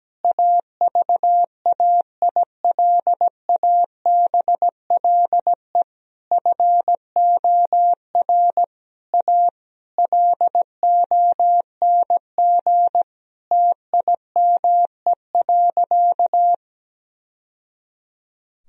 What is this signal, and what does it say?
Morse code, heard as a keyed tone, about 17 wpm, 700 Hz. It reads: AVAILABLE FOR A LONG TIME.
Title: Sentences from Top 1000 Words - Review Part 4 - 17wpm